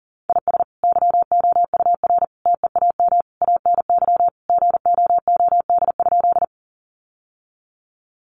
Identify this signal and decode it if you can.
Morse code, heard as a keyed tone, about 37 wpm, 700 Hz. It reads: IS YOUR TEAM ANY GOOD?
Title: Sentences from Top 1500 Words 40wpm